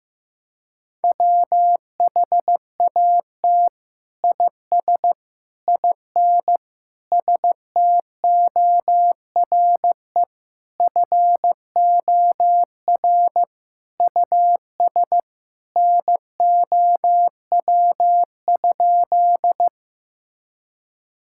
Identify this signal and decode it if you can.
Morse code, heard as a keyed tone, about 15 wpm, 700 Hz. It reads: WHAT IS IN STORE FOR US NOW?
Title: Sentences from Top 1000 Words - Review Part 5 - 15wpm